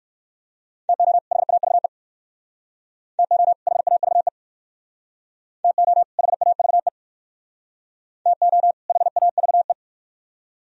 Morse code, heard as a keyed tone, about 56 wpm, 700 Hz. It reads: TO HAVE TO HAVE TO HAVE TO HAVE